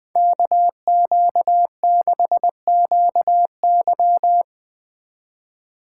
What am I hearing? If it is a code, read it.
Morse code, heard as a keyed tone, about 20 wpm, 700 Hz. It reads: KQ6QY